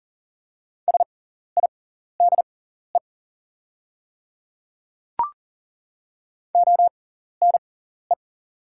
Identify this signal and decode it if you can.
Morse code, heard as a keyed tone, about 38 wpm, spaced as for 14 wpm, 700 Hz. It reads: SIDE ONE
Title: 6x Character Spacing - Top 100 Words - 40wpm